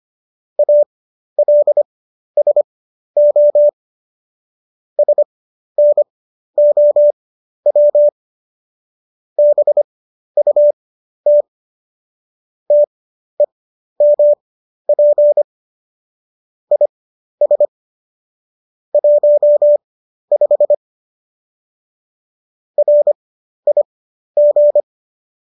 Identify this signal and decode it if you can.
Morse code, heard as a keyed tone, about 25 wpm, spaced as for 12 wpm, 590 Hz. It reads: ALSO SNOW BUT TEMP IS 15 RIG